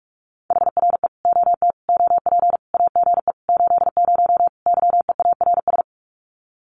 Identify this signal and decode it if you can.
Morse code, heard as a keyed tone, about 45 wpm, 700 Hz. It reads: HRE OT OP AGE 80 YEARS
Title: Common QSO Phrases 45wpm